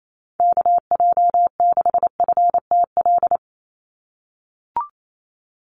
Morse code, heard as a keyed tone, about 28 wpm, 700 Hz. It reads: KJ6FTL
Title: USA Call Signs 28wpm